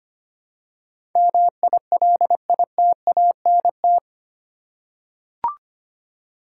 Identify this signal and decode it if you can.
Morse code, heard as a keyed tone, about 25 wpm, 700 Hz. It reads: MILITANT
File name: Eight Letter Words - 25wpm